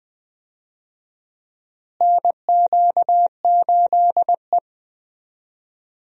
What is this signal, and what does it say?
Morse code, heard as a keyed tone, about 20 wpm, 700 Hz. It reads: NQ8E
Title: USA Call Signs - 4 Characters - Repeated 2x - 20wpm